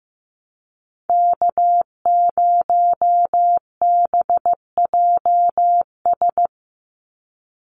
Morse code, heard as a keyed tone, about 15 wpm, 700 Hz. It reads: K0BJS
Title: USA Call Signs 15wpm